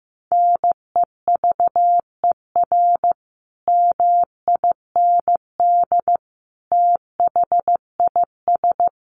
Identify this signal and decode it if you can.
Morse code, heard as a keyed tone, about 15 wpm, 700 Hz. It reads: NEVER MIND THIS
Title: Sentences from Top 1400 Words 15wpm